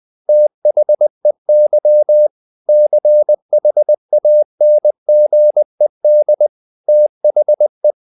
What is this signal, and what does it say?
Morse code, heard as a keyed tone, about 20 wpm, 600 Hz. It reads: THEY CHANGED THE